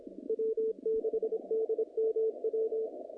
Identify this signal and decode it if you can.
Morse code, heard as a keyed tone, about 23 wpm, 440 Hz, with radio noise. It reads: W6DMW
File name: contests / naqp-cw-k5tr-op-k5pi- -0000